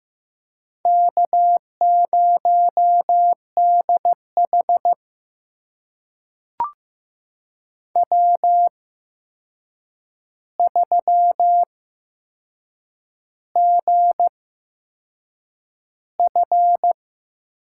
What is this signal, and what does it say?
Morse code, heard as a keyed tone, about 15 wpm, spaced as for 6 wpm, 700 Hz. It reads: K0DH W3GF
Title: USA Call Signs - 4 Characters - 8x Character Spacing - 15wpm